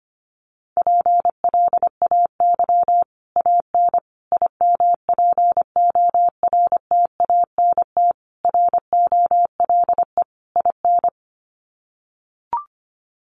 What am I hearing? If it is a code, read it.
Morse code, heard as a keyed tone, about 25 wpm, 700 Hz. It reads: PLAY AN IMPORTANT ROLE IN
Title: Sets of 5 Words Encore 25wpm